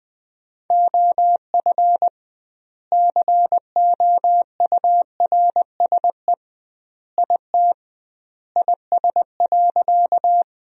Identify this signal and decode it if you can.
Morse code, heard as a keyed tone, about 20 wpm, 700 Hz. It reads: OF COURSE IT IS.
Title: 2x Word-Spacing - Sentences from Top 500 Words - 20wpm